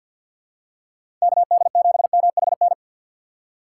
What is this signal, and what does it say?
Morse code, heard as a keyed tone, about 47 wpm, 700 Hz. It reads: KD7MHN
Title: USA Call Signs - 6 Characters - 4x Character Spacing - 50wpm